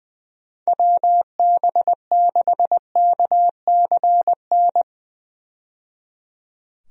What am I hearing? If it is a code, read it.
Morse code, heard as a keyed tone, about 20 wpm, 700 Hz. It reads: WB6KCN
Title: USA Call Signs - 6 Characters - 20wpm